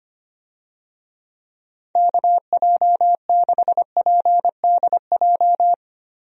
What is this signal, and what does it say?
Morse code, heard as a keyed tone, about 25 wpm, 700 Hz. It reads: KJ6PDJ
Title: USA Call Signs - 6 Characters - Repeated 2x - 25wpm